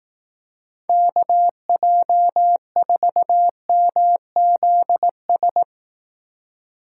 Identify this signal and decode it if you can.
Morse code, heard as a keyed tone, about 18 wpm, 700 Hz. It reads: KJ4MZS
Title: Speed Racing - USA Call Signs 15wpm